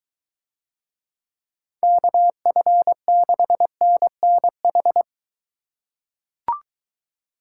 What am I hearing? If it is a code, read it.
Morse code, heard as a keyed tone, about 22 wpm, 700 Hz. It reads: KF6NNH